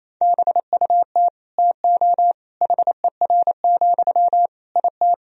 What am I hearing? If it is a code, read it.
Morse code, heard as a keyed tone, about 28 wpm, 700 Hz. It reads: BUT TO HER, IT
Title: Sentences from Top 1000 Words - Review Part 2 - 28wpm